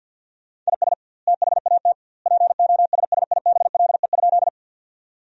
Morse code, heard as a keyed tone, about 52 wpm, 700 Hz. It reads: IS THAT POSSIBLE?
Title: 2x Word-Spacing - Sentences from Top 500 Words - 50wpm